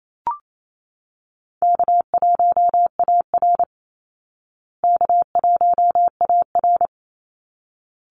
Morse code, heard as a keyed tone, about 28 wpm, 700 Hz. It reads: K1AR K1AR
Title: Easy Call Signs Repeated 2x - 28wpm